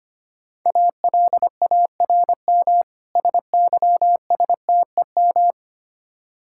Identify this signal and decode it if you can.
Morse code, heard as a keyed tone, about 26 wpm, 700 Hz. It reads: ALARM SYSTEM
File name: Easy Words - Speed-Racing with Hints - 25wpm